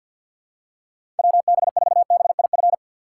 Morse code, heard as a keyed tone, about 51 wpm, 700 Hz. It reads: WB4BIF
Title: USA Call Signs - 6 Characters - 3x Character Spacing - 50wpm